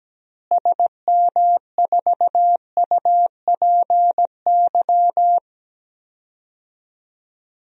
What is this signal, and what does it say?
Morse code, heard as a keyed tone, about 17 wpm, 700 Hz. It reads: SM4UPY
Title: DXCC-100 Call Signs - 6 Characters - Repeated 2x - 17wpm